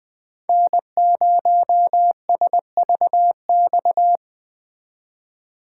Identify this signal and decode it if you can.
Morse code, heard as a keyed tone, about 21 wpm, 700 Hz. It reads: N0SVX